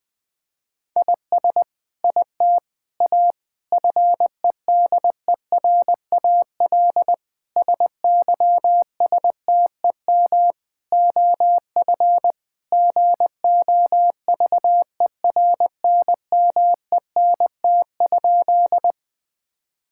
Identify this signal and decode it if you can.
Morse code, heard as a keyed tone, about 21 wpm, 700 Hz. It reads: IS IT A FEDERAL SYSTEM OF GOVERNMENT?
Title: Sentences from Top 1000 Words - Review Part 5 - 20wpm